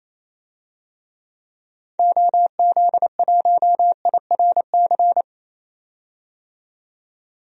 Morse code, heard as a keyed tone, about 28 wpm, 700 Hz. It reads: OZ1IRC
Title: DXCC-100 Call Signs - 6 Characters - 28wpm